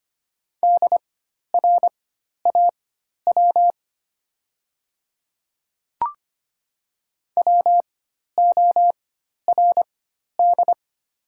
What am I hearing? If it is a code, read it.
Morse code, heard as a keyed tone, about 25 wpm, spaced as for 12 wpm, 700 Hz. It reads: DRAW WORD
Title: 4x Character Spacing - Top 500 Words - 25wpm